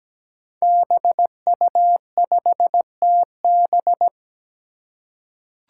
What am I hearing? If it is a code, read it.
Morse code, heard as a keyed tone, about 17 wpm, 700 Hz. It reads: BU5TB
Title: Five Challenging Letter-Number Combinations 17wpm